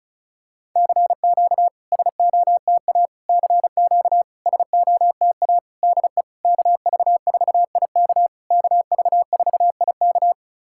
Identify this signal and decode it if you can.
Morse code, heard as a keyed tone, about 36 wpm, 690 Hz. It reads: CQ SOTA CQ SOTA DE KV4IK KV4IK
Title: SOTA QSO Practice 35wpm